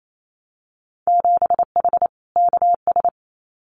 Morse code, heard as a keyed tone, about 28 wpm, 700 Hz. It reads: 7H KS
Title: Winter Field Day Exchanges Encore 28wpm